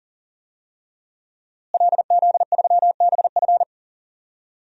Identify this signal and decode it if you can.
Morse code, heard as a keyed tone, about 37 wpm, 700 Hz. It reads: LZ3BF
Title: DXCC-100 Call Signs - 5 Characters - Repeated 3x - 40wpm